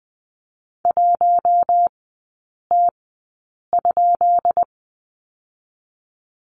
Morse code, heard as a keyed tone, about 20 wpm, spaced as for 8 wpm, 690 Hz. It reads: 1T?
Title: 212 Lesson 34e - Number 8 - Three Characters 20wpm